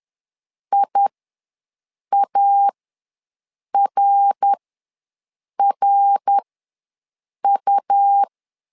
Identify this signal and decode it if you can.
Morse code, heard as a keyed tone, about 10 wpm, spaced as for 6 wpm, 800 Hz. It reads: IARRU
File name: LEZIONI / 3 MWRO / 3 MWR ws 10wpm 2151